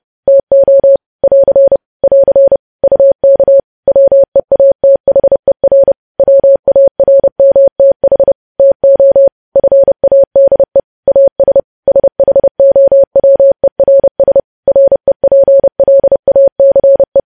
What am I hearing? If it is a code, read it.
Morse code, heard as a keyed tone, about 30 wpm, 550 Hz. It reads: TO <AR> <AR> UK WEATHER WARMTH TO FADE AS SHOWERS REPLACE
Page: news headlines in morse code at 30 wpm